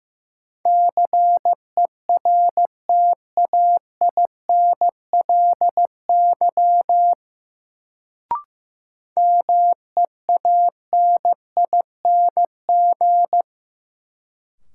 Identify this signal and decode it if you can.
Morse code, heard as a keyed tone, about 15 wpm, 700 Hz. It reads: CERTAINLY MEANING
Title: Sentences from Top 900 Words - New Word Drill - 15wpm